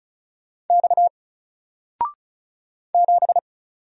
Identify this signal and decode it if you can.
Morse code, heard as a keyed tone, about 33 wpm, 700 Hz. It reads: X 7
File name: Single-Letter-Number-35wpm